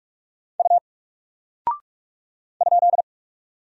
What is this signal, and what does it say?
Morse code, heard as a keyed tone, about 44 wpm, 700 Hz. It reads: U ?
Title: Single Character (Letter:Number:Punctuation:Prosign) Alt 45wpm